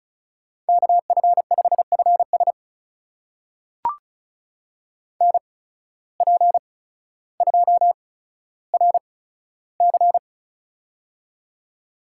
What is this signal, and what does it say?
Morse code, heard as a keyed tone, about 36 wpm, spaced as for 15 wpm, 700 Hz. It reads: KF5FS NP2RC
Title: USA Call Signs - 8x Character Spacing - 35wpm